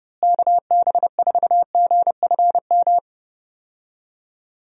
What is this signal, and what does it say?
Morse code, heard as a keyed tone, about 32 wpm, 700 Hz. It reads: KB4GFM
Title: USA Call Signs - 6 Characters - Repeated 2x - 30wpm